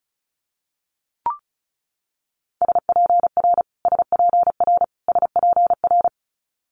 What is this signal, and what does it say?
Morse code, heard as a keyed tone, about 36 wpm, 700 Hz. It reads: SPR SPR SPR
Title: Multi-letter Phonemes Repeated 3x 35wpm